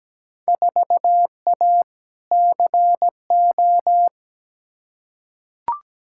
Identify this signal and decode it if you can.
Morse code, heard as a keyed tone, about 17 wpm, 700 Hz. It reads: 4A CO
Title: Field Day Exchanges - 17wpm